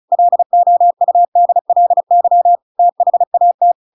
Morse code, heard as a keyed tone, about 34 wpm, 700 Hz. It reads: LOUDLY THAT
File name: Pinocchio / Pinocchio ch7 35 35 ik1hge